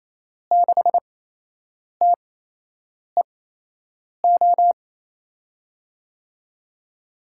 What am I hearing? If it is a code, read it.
Morse code, heard as a keyed tone, about 28 wpm, spaced as for 8 wpm, 700 Hz. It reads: 6TEO
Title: USA Call Signs - 6 Characters - 8x Character Spacing - 28wpm